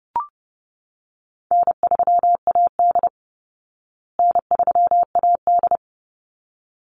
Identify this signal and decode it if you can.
Morse code, heard as a keyed tone, about 30 wpm, 700 Hz. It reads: N3AD N3AD
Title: Easy Call Signs Repeated 2x - 30wpm